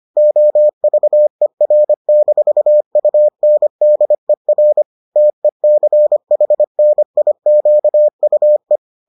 Morse code, heard as a keyed tone, about 25 wpm, 600 Hz. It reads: OVER-UNDER TECHNIQUE